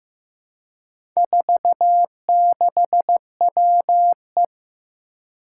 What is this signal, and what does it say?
Morse code, heard as a keyed tone, about 15 wpm, 700 Hz. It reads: 46WE
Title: Four Challenging Letter-Number Combinations 15wpm